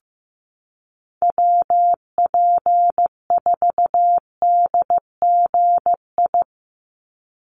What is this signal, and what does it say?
Morse code, heard as a keyed tone, about 15 wpm, 700 Hz. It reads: WP4DGI
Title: USA Call Signs 15wpm